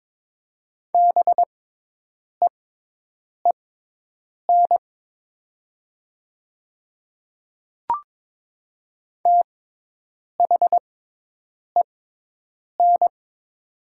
Morse code, heard as a keyed tone, about 22 wpm, spaced as for 8 wpm, 700 Hz. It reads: BEEN THEN
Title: 6x Character Spacing - Top 100 Words - 22wpm